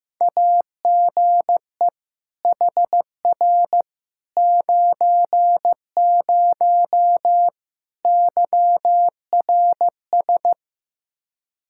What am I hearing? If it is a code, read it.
Morse code, heard as a keyed tone, about 15 wpm, 700 Hz. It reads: AGE HR 90 YRS
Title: K1USN SST QSO Practice 15wpm